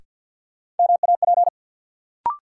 Morse code, heard as a keyed tone, about 53 wpm, 700 Hz. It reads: MAP
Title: Sentences from Top 400 Words - New Word Drill - 50wpm